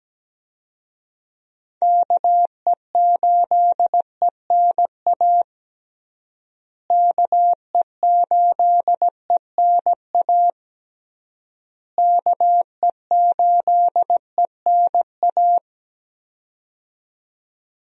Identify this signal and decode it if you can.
Morse code, heard as a keyed tone, about 17 wpm, 700 Hz. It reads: KE8ENA KE8ENA KE8ENA
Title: USA Call Signs - 6 Characters - Repeated 3x - 17wpm